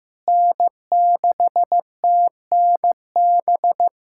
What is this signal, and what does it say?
Morse code, heard as a keyed tone, about 15 wpm, 700 Hz. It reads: N6TNB
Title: USA Callsign DE Callsign - 5 Characters - 15wpm